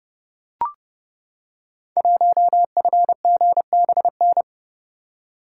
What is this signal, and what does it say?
Morse code, heard as a keyed tone, about 30 wpm, 700 Hz. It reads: 1FGBN
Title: Code Groups - 5 Letter:Numbers - 30wpm